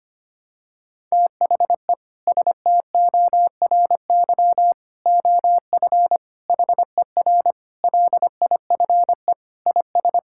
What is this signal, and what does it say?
Morse code, heard as a keyed tone, about 25 wpm, 700 Hz. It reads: THE STORY OF HER LIFE IS
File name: practice / 25wpm / Sentences with 3-Word Phrases - 25wpm